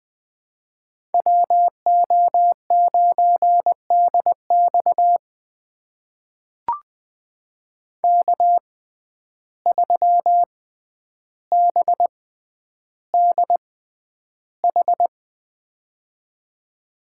Morse code, heard as a keyed tone, about 20 wpm, spaced as for 10 wpm, 700 Hz. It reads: WO9DX K3BDH
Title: USA Call Signs - 5 Characters - 6x Character Spacing - 20wpm